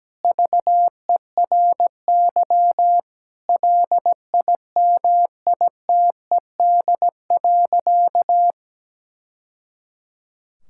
Morse code, heard as a keyed tone, about 17 wpm, 700 Hz. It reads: VERY LIMITED.
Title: Sentences from Top 1000 Words - Review Part 6 - 17wpm